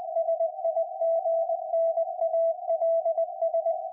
Morse code, heard as a keyed tone, about 20 wpm, 660 Hz. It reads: SIGNALS
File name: IC-7100